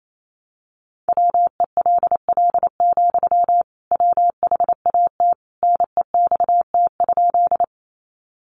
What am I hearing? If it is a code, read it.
Morse code, heard as a keyed tone, about 28 wpm, 700 Hz. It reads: WELL, WHAT NEXT?
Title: Sentences from Top 300 Words 28wpm